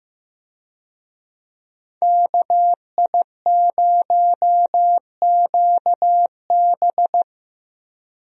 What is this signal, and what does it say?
Morse code, heard as a keyed tone, about 15 wpm, 700 Hz. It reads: KI0QB
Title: USA Call Signs - 5 Characters - 15wpm